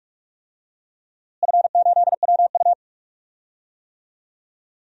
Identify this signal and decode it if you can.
Morse code, heard as a keyed tone, about 46 wpm, 700 Hz. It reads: F8WU
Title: DXCC-100 Call Signs - 4 Characters - Repeated 2x - 45wpm